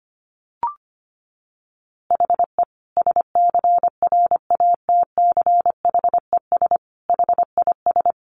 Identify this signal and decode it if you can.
Morse code, heard as a keyed tone, about 25 wpm, 700 Hz. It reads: HE SCRATCHES HIS
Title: Easy Phrases 25wpm